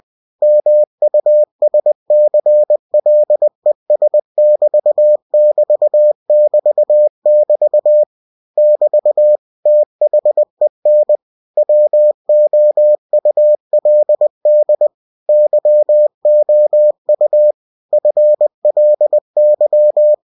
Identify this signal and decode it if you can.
Morse code, heard as a keyed tone, about 20 wpm, 600 Hz. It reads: MUSCLES<BT><BT><BT><BT> <BT> THEN WOULD YOU FLY
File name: cw-books / winnie-the-pooh / winnie-20wpm-page-0098